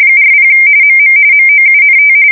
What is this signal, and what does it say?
RTTY, about 45.45 baud. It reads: 40 MM MOUNT
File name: fsk170 / USS-Missouri